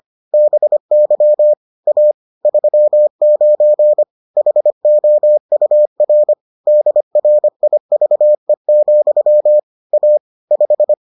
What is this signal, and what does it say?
Morse code, heard as a keyed tone, about 25 wpm, 600 Hz. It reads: BY A 39 HOUR DRIVE, A 5